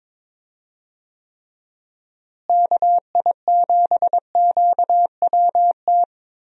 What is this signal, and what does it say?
Morse code, heard as a keyed tone, about 22 wpm, 700 Hz. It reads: KI7QWT
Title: USA Call Signs - 6 Characters - Repeated 2x - 22wpm